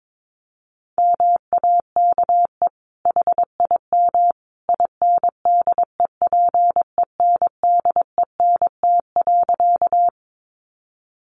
Morse code, heard as a keyed tone, about 22 wpm, 700 Hz. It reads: MAKE HIM INDEPENDENT.
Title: Sentences from Top 900 Words 22wpm